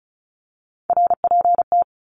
Morse code, heard as a keyed tone, about 34 wpm, 700 Hz. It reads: RPT
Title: Abbreviations Repeated 3x - 35wpm